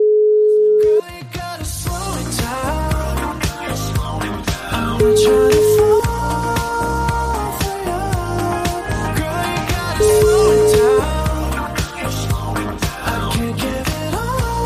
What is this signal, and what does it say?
Ringback tone (Europe, 425 Hz), with noise.